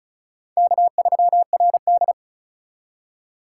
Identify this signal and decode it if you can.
Morse code, heard as a keyed tone, about 34 wpm, 700 Hz. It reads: K3RD
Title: USA Call Signs - 4 Characters - 8x Character Spacing - 35wpm